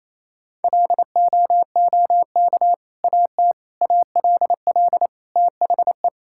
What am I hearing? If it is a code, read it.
Morse code, heard as a keyed tone, about 28 wpm, 700 Hz. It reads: LOOK AT ALL THE
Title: Sentences from Top 1000 Words - Review Part 5 - 28wpm